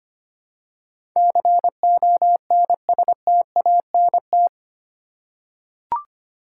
Morse code, heard as a keyed tone, about 25 wpm, 700 Hz. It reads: CONSTANT
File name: Eight Letter Words - 25wpm